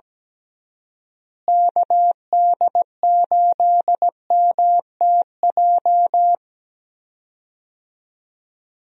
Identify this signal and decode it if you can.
Morse code, heard as a keyed tone, about 17 wpm, 700 Hz. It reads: KD8MTJ
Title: USA Call Signs - 6 Characters - Repeated 3x - 17wpm